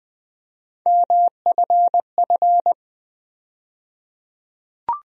Morse code, heard as a keyed tone, about 20 wpm, 700 Hz. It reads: MFF
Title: USA Call Sign Postfixes - 3 Letters 20wpm